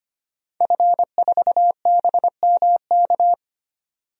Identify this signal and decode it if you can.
Morse code, heard as a keyed tone, about 25 wpm, 700 Hz. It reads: F4BMK
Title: DXCC-100 Call Signs - 5 Characters - 25wpm